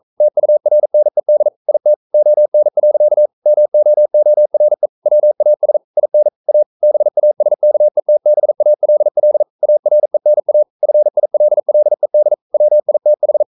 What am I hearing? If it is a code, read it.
Morse code, heard as a keyed tone, about 42 wpm, 600 Hz. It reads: TURNED IT ON. MOORE WAS IN A BASKETBALL ARENA FILLED WITH
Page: Romanen Never af Ken Follet på engelsk.